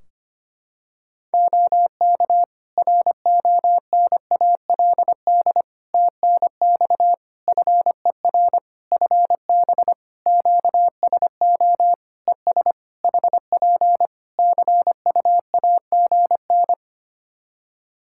Morse code, heard as a keyed tone, about 25 wpm, 700 Hz. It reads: OK RONALD TNX FER FB QSO ES HP CUAGN
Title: Practice QSO - Exchange 4 - Ending - 25wpm